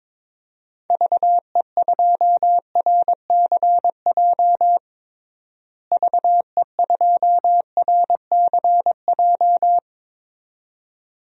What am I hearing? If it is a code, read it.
Morse code, heard as a keyed tone, about 21 wpm, 700 Hz. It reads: VE2RCJ VE2RCJ
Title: DXCC-100 Call Signs - 6 Characters - Repeated 2x - 22wpm